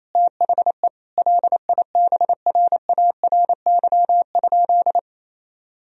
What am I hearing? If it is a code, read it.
Morse code, heard as a keyed tone, about 28 wpm, 700 Hz. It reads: THE LIBRARY?
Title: Sentences from Top 1000 Words - Review Part 4 - 28wpm